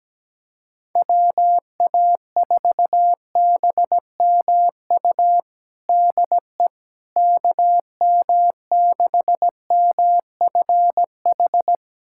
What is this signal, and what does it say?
Morse code, heard as a keyed tone, about 17 wpm, 700 Hz. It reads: WA4BMU DE KM6MFH
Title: USA Callsign DE Callsign 17wpm